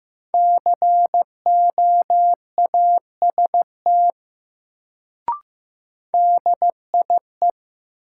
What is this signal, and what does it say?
Morse code, heard as a keyed tone, about 15 wpm, 700 Hz. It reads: COAST DIE
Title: Sentences from Top 1300 Words - New Word Drill Alt - 15wpm